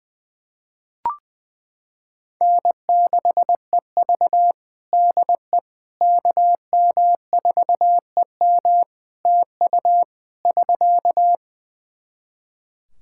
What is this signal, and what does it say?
Morse code, heard as a keyed tone, about 20 wpm, 700 Hz. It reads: N6EV DE KM4EM TU <SK>
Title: Practice QSO - Exchange 4 - Ending - 20wpm